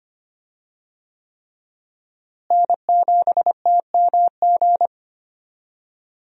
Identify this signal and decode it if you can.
Morse code, heard as a keyed tone, about 25 wpm, 700 Hz. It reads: N7TMG
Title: USA Call Signs - 5 Characters - Repeated 3x - 25wpm